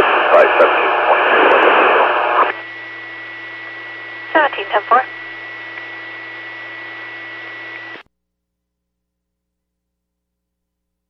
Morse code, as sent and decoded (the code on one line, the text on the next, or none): none